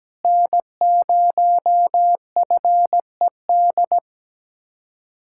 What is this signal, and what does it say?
Morse code, heard as a keyed tone, about 17 wpm, 700 Hz. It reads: N0FED
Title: K1USN SST QSO Practice - 17wpm